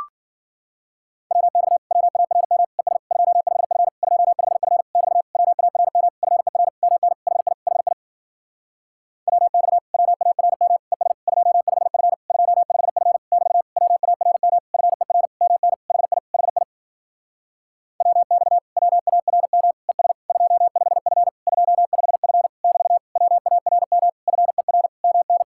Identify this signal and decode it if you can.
Morse code, heard as a keyed tone, about 52 wpm, 700 Hz. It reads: WX WARM ES 25F 25F <BT> WARM FER MN HI HI WX WARM ES 25F 25F <BT> WARM FER MN HI HI WX WARM ES 25F 25F <BT> WARM FER MN